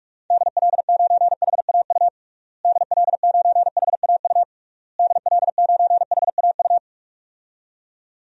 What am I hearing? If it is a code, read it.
Morse code, heard as a keyed tone, about 46 wpm, 700 Hz. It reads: DL9HAU DL9HAU DL9HAU
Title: DXCC-100 Call Signs - 6 Characters - Repeated 3x - 45wpm